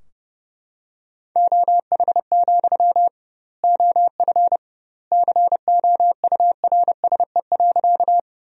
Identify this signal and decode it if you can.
Morse code, heard as a keyed tone, about 32 wpm, 700 Hz. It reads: OH, OF COURSE.
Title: 2x Word-Spacing - Sentences from Top 500 Words - 30wpm